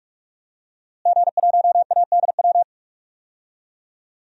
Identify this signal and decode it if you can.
Morse code, heard as a keyed tone, about 47 wpm, 700 Hz. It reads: G1ADW
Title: DXCC-100 Call Signs - 5 Characters - Repeated 2x - 45wpm